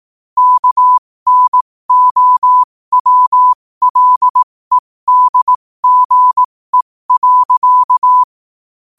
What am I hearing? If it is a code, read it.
Morse code, heard as a keyed tone, about 18 wpm, spaced as for 15 wpm, 1000 Hz. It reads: KNOWLEDGE.
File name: qotdfiles / 15wpm